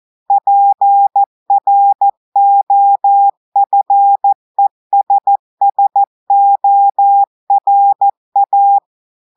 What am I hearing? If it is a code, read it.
Morse code, heard as a keyed tone, about 14 wpm, 800 Hz. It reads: PROFESSORA
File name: audios / portugais / portugais texte en clair 1 vitesse 14 mots minutes